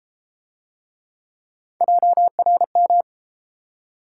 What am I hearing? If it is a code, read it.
Morse code, heard as a keyed tone, about 32 wpm, 700 Hz. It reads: JRM